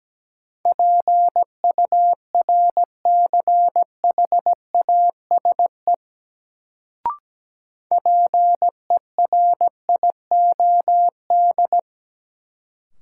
Morse code, heard as a keyed tone, about 17 wpm, 700 Hz. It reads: PURCHASE PERIOD
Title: Sentences from Top 800 Words - New Word Drill - 17wpm